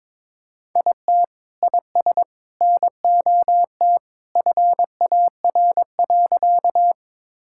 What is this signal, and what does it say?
Morse code, heard as a keyed tone, about 22 wpm, 700 Hz. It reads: IT IS NOT FAR.
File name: practice / 22wpm / Sentences with 3-Word Phrases - 22wpm